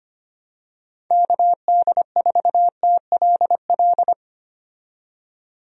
Morse code, heard as a keyed tone, about 25 wpm, 700 Hz. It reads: KD4TLL
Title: USA Call Signs - 6 Characters - 25wpm